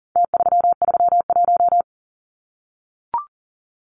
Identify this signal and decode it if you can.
Morse code, heard as a keyed tone, about 40 wpm, 700 Hz. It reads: T331
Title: Numbers - Contest Style - 4-Digit 40wpm